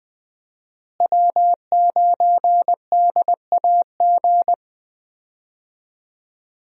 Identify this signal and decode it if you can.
Morse code, heard as a keyed tone, about 20 wpm, 700 Hz. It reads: W9DAG
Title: USA Call Signs - 5 Characters - Repeated 3x - 20wpm